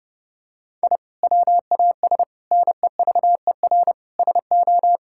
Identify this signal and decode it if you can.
Morse code, heard as a keyed tone, about 30 wpm, 700 Hz. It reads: I WAS NEVER SO
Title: Sentences from Top 1000 Words - Review Part 5 - 30wpm